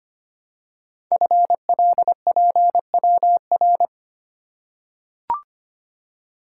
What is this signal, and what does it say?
Morse code, heard as a keyed tone, about 25 wpm, 700 Hz. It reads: FLPWR
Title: Code Groups - 5 Letters - 25wpm